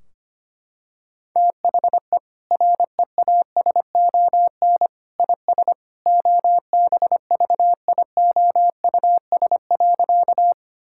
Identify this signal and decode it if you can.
Morse code, heard as a keyed tone, about 25 wpm, 700 Hz. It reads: THE REASON IS OBVIOUS.
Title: Sentences from Top 1600 Words 25wpm